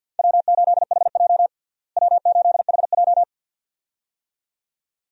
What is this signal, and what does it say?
Morse code, heard as a keyed tone, about 47 wpm, 700 Hz. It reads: W8HJ W8HJ
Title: USA Call Signs - 4 Characters - Repeated 2x - 50wpm